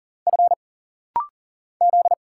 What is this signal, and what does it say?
Morse code, heard as a keyed tone, about 40 wpm, 700 Hz. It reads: F Z
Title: Single Letters - Rapid-Fire 40wpm